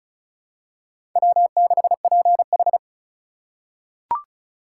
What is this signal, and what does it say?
Morse code, heard as a keyed tone, about 34 wpm, 700 Hz. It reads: W6PH
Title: USA Call Signs - 4 Characters - 3x Character Spacing - 35wpm